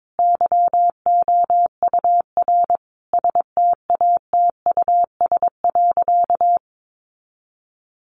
Morse code, heard as a keyed tone, about 22 wpm, 700 Hz. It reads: YOUR STATUS.
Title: Sentences from Top 1400 Words 22wpm